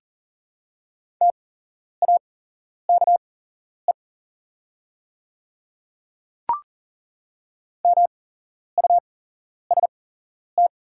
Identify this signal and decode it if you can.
Morse code, heard as a keyed tone, about 37 wpm, spaced as for 11 wpm, 690 Hz. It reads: TAKE MUST
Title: 8x Character Spacing - Top 200 Words - 40wpm